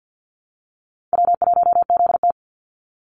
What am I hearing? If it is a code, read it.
Morse code, heard as a keyed tone, about 53 wpm, 700 Hz. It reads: W1ZT
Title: Easy Call Signs Repeated 2x - 50wpm